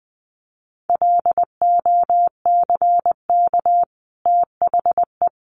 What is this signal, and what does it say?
Morse code, heard as a keyed tone, about 20 wpm, 700 Hz. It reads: LOCK THE
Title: Sentences from Top 1300 Words 20wpm